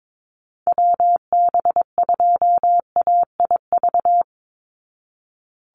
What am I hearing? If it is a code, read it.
Morse code, heard as a keyed tone, about 22 wpm, 700 Hz. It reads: WB2AIV
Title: Call Signs Repeated 3x - 22wpm